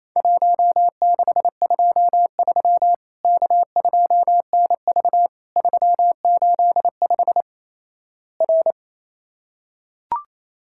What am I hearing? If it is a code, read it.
Morse code, heard as a keyed tone, about 27 wpm, 700 Hz. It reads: 1623 K2NV 385 R
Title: Worked All Europe DX Contest - QTC Exchange - 28wpm